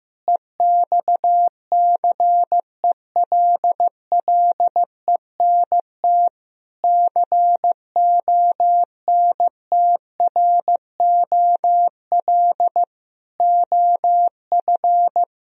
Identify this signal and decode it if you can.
Morse code, heard as a keyed tone, about 15 wpm, 700 Hz. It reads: EXCELLENT CONTROL OF
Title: Sentences from Top 1000 Words - Review Part 3 - 15wpm